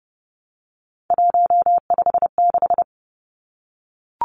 Morse code, heard as a keyed tone, about 30 wpm, 700 Hz. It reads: 156
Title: Numbers - 3-Digit 30wpm